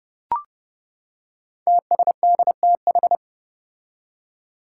Morse code, heard as a keyed tone, about 30 wpm, 700 Hz. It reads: TSDTH
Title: Five Challenging Letter-Number Combinations 30wpm